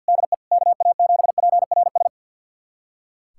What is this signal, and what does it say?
Morse code, heard as a keyed tone, about 46 wpm, 700 Hz. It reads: DE KA7PRS
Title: USA Callsign DE Callsign - 6 Characters - 50wpm